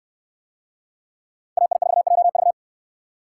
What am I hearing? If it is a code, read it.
Morse code, heard as a keyed tone, about 69 wpm, 700 Hz. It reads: AI4JV